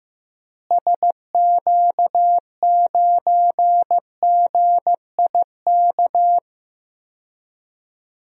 Morse code, heard as a keyed tone, about 15 wpm, 700 Hz. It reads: SQ9GIK
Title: DXCC-100 Call Signs - 6 Characters - Repeated 3x - 15wpm